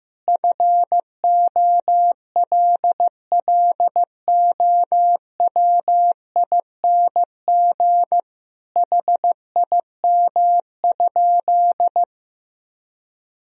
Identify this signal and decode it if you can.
Morse code, heard as a keyed tone, about 16 wpm, 700 Hz. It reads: FOLLOWING HIM?
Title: Sentences from Top 1000 Words - Review Part 4 - 15wpm